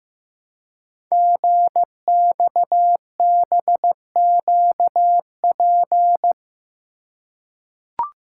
Morse code, heard as a keyed tone, about 15 wpm, 700 Hz. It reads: GXBQP
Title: Code Groups - 5 Letters - 15wpm